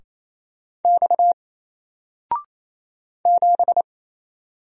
Morse code, heard as a keyed tone, about 27 wpm, 700 Hz. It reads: X 7
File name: Single-Letter-Number-28wpm